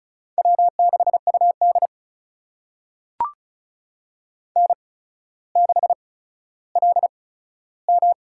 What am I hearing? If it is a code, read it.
Morse code, heard as a keyed tone, about 36 wpm, spaced as for 15 wpm, 700 Hz. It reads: W6UD N6LM
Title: USA Call Signs - 4 Characters - 8x Character Spacing - 35wpm